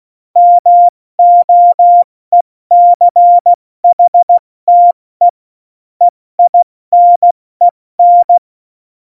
Morse code, heard as a keyed tone, about 16 wpm, 700 Hz. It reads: MOECHTE EINEN
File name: Morsekurs / Witze / Witz 20. November 2018, 70 F80